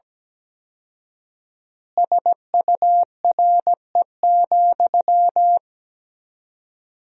Morse code, heard as a keyed tone, about 17 wpm, 700 Hz. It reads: SURE,
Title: 4x Word-Spacing - Sentences from Top 500 Words - 17wpm